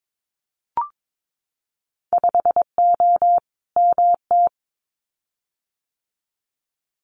Morse code, heard as a keyed tone, about 22 wpm, 700 Hz. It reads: 5O MT
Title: Winter Field Day Exchanges Encore 22wpm